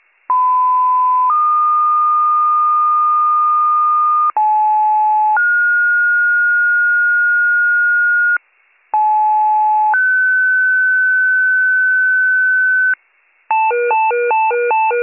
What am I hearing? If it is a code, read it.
Two-tone page, heard as a one-second tone, then a three-second tone, about 1010 then 1220 Hz, about 860 then 1430 Hz, about 860 then 1500 Hz, with noise. It is heard from 0.3 to 12.9 s.